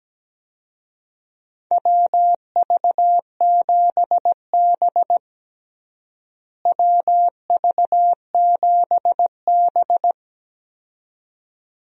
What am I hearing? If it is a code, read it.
Morse code, heard as a keyed tone, about 17 wpm, 700 Hz. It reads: WV7B WV7B
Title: USA Call Signs - 4 Characters - Repeated 2x - 17wpm